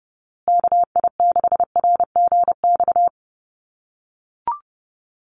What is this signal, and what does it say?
Morse code, heard as a keyed tone, about 30 wpm, 700 Hz. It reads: KI6RGX
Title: USA Call Signs Repeated 2x 30wpm